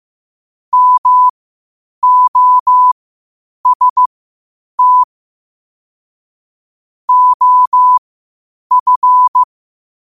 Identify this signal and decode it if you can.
Morse code, heard as a keyed tone, about 15 wpm, spaced as for 8 wpm, 1000 Hz. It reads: MOST OF
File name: qotdfiles / 8wpm